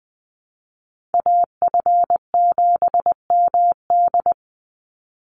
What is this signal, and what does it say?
Morse code, heard as a keyed tone, about 20 wpm, 700 Hz. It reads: AF7MD
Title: NA SOTA Chaser Callsigns 20wpm